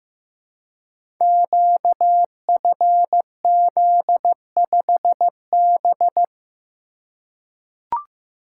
Morse code, heard as a keyed tone, about 15 wpm, 700 Hz. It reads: QFZ5B
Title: Code Groups - 5 Letter:Numbers - 15wpm